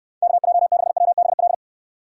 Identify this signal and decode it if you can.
Morse code, heard as a keyed tone, about 69 wpm, 700 Hz. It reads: KJ6WBL